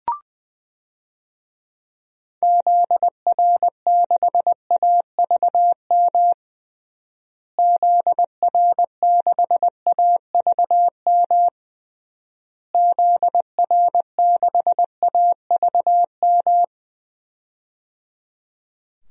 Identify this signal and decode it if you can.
Morse code, heard as a keyed tone, about 21 wpm, 700 Hz. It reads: ZR6AVM ZR6AVM ZR6AVM
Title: DXCC-100 Call Signs - 6 Characters - Repeated 3x - 20wpm